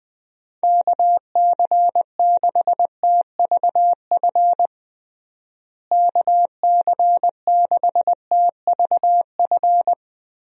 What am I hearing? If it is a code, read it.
Morse code, heard as a keyed tone, about 21 wpm, 700 Hz. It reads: KC6TVF KC6TVF
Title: USA Call Signs - 6 Characters - Repeated 2x - 20wpm